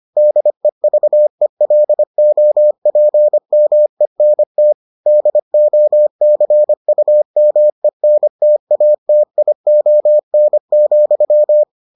Morse code, heard as a keyed tone, about 25 wpm, 600 Hz. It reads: DEVELOPMENT DOCUMENTATION,